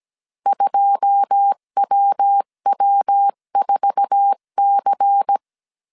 Morse code, heard as a keyed tone, about 17 wpm, 800 Hz. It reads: 2WW4C
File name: CW / DET / 1 9 1500C S B 16wpm 2856